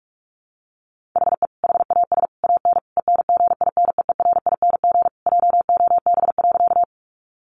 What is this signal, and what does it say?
Morse code, heard as a keyed tone, about 44 wpm, 700 Hz. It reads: HE HAS AN ENGINEERING JOB.
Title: Sentences from Top 1500 Words 45wpm